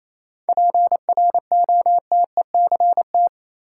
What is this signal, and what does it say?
Morse code, heard as a keyed tone, about 28 wpm, 700 Hz. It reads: PROTECT
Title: Sentences from Top 900 Words - New Word Drill - 28wpm